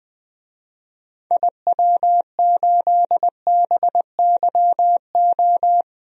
Morse code, heard as a keyed tone, about 21 wpm, 700 Hz. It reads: IW8BYO
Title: DXCC-100 Call Signs - 6 Characters - 20wpm